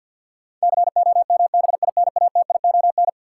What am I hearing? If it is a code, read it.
Morse code, heard as a keyed tone, about 52 wpm, 700 Hz. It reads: COMBINATION
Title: Sentences from Top 1000 Words - Review Part 6 - New Word Drill - 50wpm